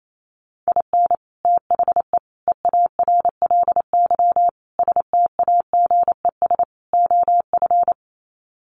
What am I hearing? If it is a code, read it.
Morse code, heard as a keyed tone, about 28 wpm, 700 Hz. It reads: IN THE EARLY STAGES OF
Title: Sets of 5 Words Encore 28wpm